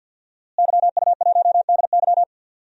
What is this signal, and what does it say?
Morse code, heard as a keyed tone, about 46 wpm, 700 Hz. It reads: YU1DY
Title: DXCC-100 Call Signs - 5 Characters - Repeated 3x - 50wpm